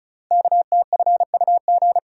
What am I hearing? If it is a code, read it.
Morse code, heard as a keyed tone, about 33 wpm, 700 Hz. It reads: KTFUG
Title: Code Groups - 5 Letters - 35wpm